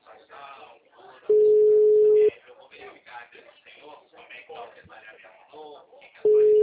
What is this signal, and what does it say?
Ringback tone (Europe, 425 Hz), with speech.
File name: out- -2004- - - .1178